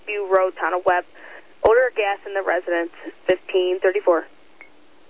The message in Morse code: none

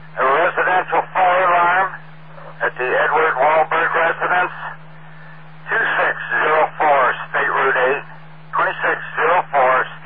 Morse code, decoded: none

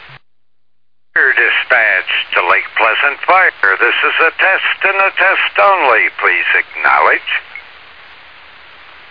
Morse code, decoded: none

